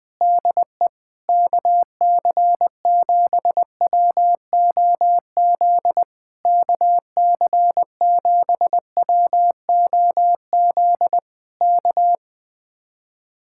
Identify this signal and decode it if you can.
Morse code, heard as a keyed tone, about 20 wpm, 700 Hz. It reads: DE KC7WOZ KC7WOZ K
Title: Speed-Racing Common QSO Phrases - 20wpm